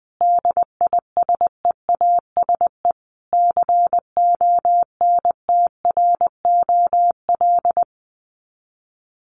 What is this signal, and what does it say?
Morse code, heard as a keyed tone, about 20 wpm, 700 Hz. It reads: DISEASE CONTROL
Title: Sets of 5 Words Encore2 20wpm